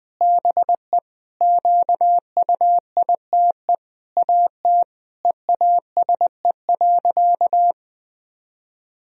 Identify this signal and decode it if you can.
Morse code, heard as a keyed tone, about 20 wpm, 700 Hz. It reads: BE QUITE AT EASE.
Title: Sentences from Top 1000 Words - Review Part 1 - 20wpm